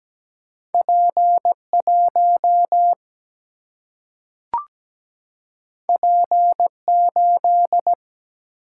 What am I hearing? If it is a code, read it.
Morse code, heard as a keyed tone, about 17 wpm, 700 Hz. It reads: P1 P8
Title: Single-Letter Single-Number 17wpm